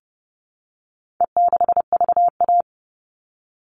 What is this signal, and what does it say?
Morse code, heard as a keyed tone, about 28 wpm, 700 Hz. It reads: E6VA
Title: All Cut Numbers - 4-Digit 30wpm